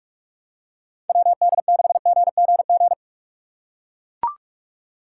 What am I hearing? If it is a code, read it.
Morse code, heard as a keyed tone, about 41 wpm, 700 Hz. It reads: WD6GGG
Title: USA Call Signs - 4x Character Spacing - 45wpm